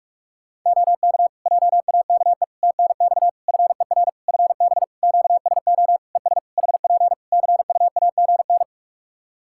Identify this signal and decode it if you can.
Morse code, heard as a keyed tone, about 46 wpm, 700 Hz. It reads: OK JAKE TNX FER FB QSO ES HP CUAGN
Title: Practice QSO - Exchange 4 - Ending - 45wpm